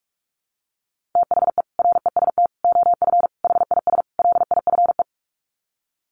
Morse code, heard as a keyed tone, about 44 wpm, 700 Hz. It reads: THE REST OF HIS LIFE
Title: Sets of 5 Words 45wpm